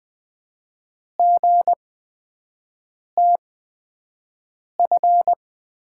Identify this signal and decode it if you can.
Morse code, heard as a keyed tone, about 20 wpm, spaced as for 5 wpm, 700 Hz. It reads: GTF
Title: USA Call Signs - 8x Character Spacing - 20wpm